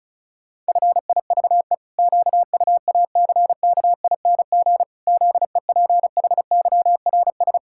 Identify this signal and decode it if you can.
Morse code, heard as a keyed tone, about 36 wpm, 700 Hz. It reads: FIVE QUACKING ZEPHYRS
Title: Pangram Challenge v2 - 35wpm